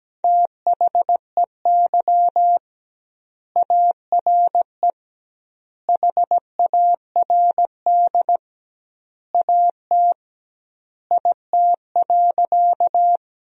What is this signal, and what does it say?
Morse code, heard as a keyed tone, about 17 wpm, 700 Hz. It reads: THEY ARE HARD AT IT.
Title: 2x Word-Spacing - Sentences from Top 300 Words - 17wpm